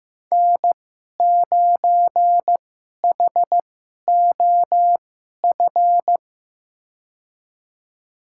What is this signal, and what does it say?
Morse code, heard as a keyed tone, about 15 wpm, spaced as for 11 wpm, 700 Hz. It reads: N9HOF
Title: USA Call Signs - 5 Characters - 2x Character Spacing - 15wpm